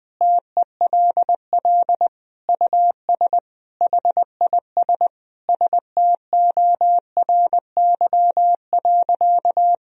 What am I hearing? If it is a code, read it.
Morse code, heard as a keyed tone, about 21 wpm, 700 Hz. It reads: TELL US HIS STORY.
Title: Sentences from Top 1000 Words - Review Part 1 - 20wpm